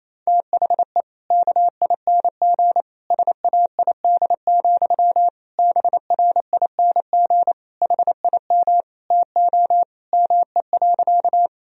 Morse code, heard as a keyed tone, about 28 wpm, 700 Hz. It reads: THE KING SAID, BRING HIM TO ME.
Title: Sentences from Top 1000 Words - Review Part 3 - 28wpm